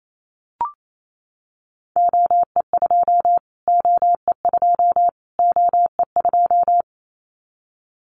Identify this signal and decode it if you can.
Morse code, heard as a keyed tone, about 28 wpm, 700 Hz. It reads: OE2 OE2 OE2
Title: Callsign Prefixes for Top 25 DXCC - Repeated 3x - 28wpm